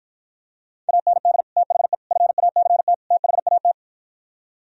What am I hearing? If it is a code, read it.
Morse code, heard as a keyed tone, about 52 wpm, 700 Hz. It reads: AND THE FACT THAT